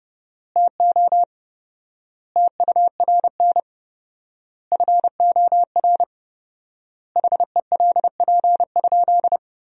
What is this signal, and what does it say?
Morse code, heard as a keyed tone, about 30 wpm, 700 Hz. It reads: TO TURN FOR HELP?
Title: 4x Word-Spacing - Sentences from Top 200 Words - 30wpm